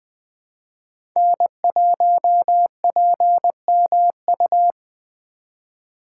Morse code, heard as a keyed tone, about 20 wpm, 700 Hz. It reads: N1PMU
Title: USA Call Signs - 5 Characters - 3x Character Spacing - 20wpm